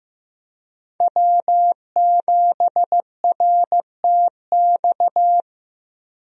Morse code, heard as a keyed tone, about 15 wpm, 700 Hz. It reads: W7RTX
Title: USA Call Signs - 5 Characters - 15wpm